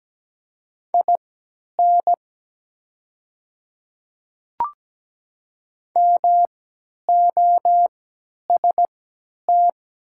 Morse code, heard as a keyed tone, about 17 wpm, spaced as for 10 wpm, 700 Hz. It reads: IN MOST
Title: 3x Character Spacing - Top 100 Words - 17wpm